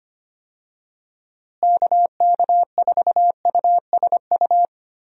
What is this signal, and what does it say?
Morse code, heard as a keyed tone, about 25 wpm, 700 Hz. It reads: KK4USU
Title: USA Call Signs - 6 Characters - Repeated 3x - 25wpm